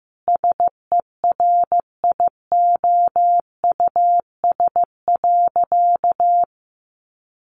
Morse code, heard as a keyed tone, about 15 wpm, 700 Hz. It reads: SERIOUS.
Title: Sentences from Top 1100 Words 15wpm